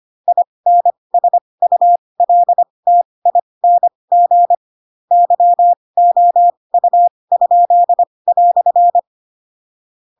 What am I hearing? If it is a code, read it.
Morse code, heard as a keyed tone, about 25 wpm, spaced as for 20 wpm, 700 Hz. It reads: INSULTING YOU?"
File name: Pinocchio / Pinocchio ch2 15 25 ik1hge